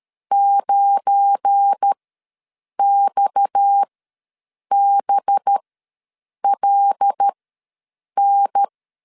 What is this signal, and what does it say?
Morse code, heard as a keyed tone, about 13 wpm, spaced as for 7 wpm, 800 Hz. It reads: 9XBLN
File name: LEZIONI / 7 GF9Z / 7 GF9 ws 12wpm 3344